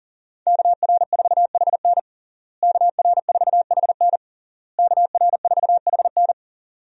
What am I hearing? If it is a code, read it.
Morse code, heard as a keyed tone, about 41 wpm, 700 Hz. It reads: KR4HN KR4HN KR4HN
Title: USA Call Signs - 5 Characters - Repeated 3x - 40wpm